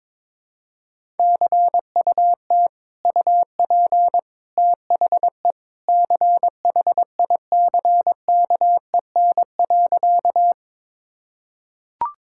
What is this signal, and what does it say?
Morse code, heard as a keyed tone, about 22 wpm, 700 Hz. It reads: CUT UP THE CHICKEN.
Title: Sentences from Top 1000 Words - Review Part 5 - 22wpm